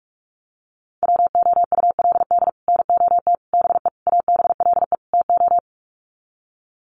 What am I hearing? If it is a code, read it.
Morse code, heard as a keyed tone, about 44 wpm, 700 Hz. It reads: WOULD NOT BE ABLE TO
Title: Sets of 5 Words 45wpm